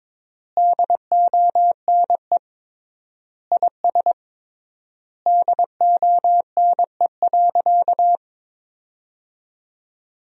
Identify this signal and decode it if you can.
Morse code, heard as a keyed tone, about 22 wpm, 700 Hz. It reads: DONE IS DONE.
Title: 3x Word-Spacing - Sentences from Top 500 Words - 22wpm